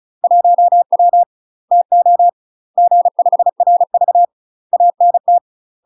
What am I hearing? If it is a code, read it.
Morse code, heard as a keyed tone, about 34 wpm, 700 Hz. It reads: 1W TO G5RV ANT
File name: UK MASTER QSOs TRAINING / 35WPM / UK MASTER QSOs 0021